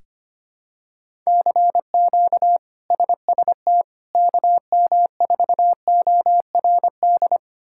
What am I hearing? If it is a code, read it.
Morse code, heard as a keyed tone, about 25 wpm, 700 Hz. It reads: CQ SST KM4ORD
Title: K1USN SST QSO Practice - 25wpm